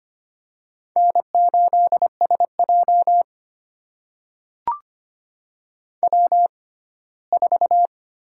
Morse code, heard as a keyed tone, about 25 wpm, spaced as for 17 wpm, 700 Hz. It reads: N8SJ W4
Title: USA Call Signs - 4 Characters - 6x Character Spacing - 25wpm